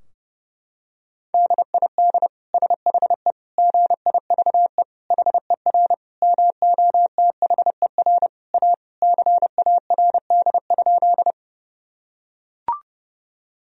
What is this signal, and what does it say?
Morse code, heard as a keyed tone, about 32 wpm, 700 Hz. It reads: DID SHE GIVE HER MOTHER A CARD?
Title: Sentences from Top 1000 Words - Review Part 4 - 30wpm